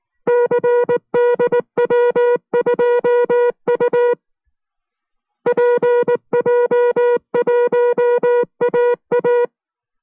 Morse code, heard as a keyed tone, about 19 wpm, 480 Hz. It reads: CDW2U PJ1AA